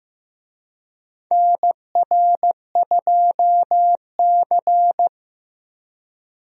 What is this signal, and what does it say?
Morse code, heard as a keyed tone, about 15 wpm, 700 Hz. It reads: NR2C
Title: USA Call Signs - 4 Characters - Repeated 3x - 15wpm